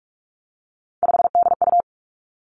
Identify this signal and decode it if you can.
Morse code, heard as a keyed tone, about 46 wpm, 700 Hz. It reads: 5DU
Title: Three Challenging Letter-Number Combinations 45wpm